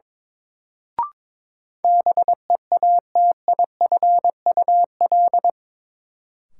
Morse code, heard as a keyed tone, about 22 wpm, 700 Hz. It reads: BEATIFUL
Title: Sentences from Top 1300 Words - New Word Drill Alt - 22wpm